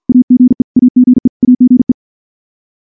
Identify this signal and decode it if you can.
Morse code, heard as a keyed tone, about 50 wpm, 250 Hz. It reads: AGE AGE AGE
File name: morse / morse-training-50WPM